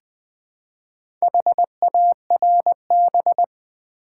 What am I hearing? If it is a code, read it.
Morse code, heard as a keyed tone, about 20 wpm, 700 Hz. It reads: HARB
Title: Build A Word - 2 Letters At A Time - 6-Letter Words - 20wpm